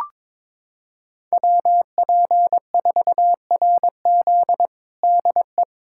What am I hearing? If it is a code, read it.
Morse code, heard as a keyed tone, about 22 wpm, 700 Hz. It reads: WP4RZ DE
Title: USA Callsign DE Callsign - 5 Characters - 22wpm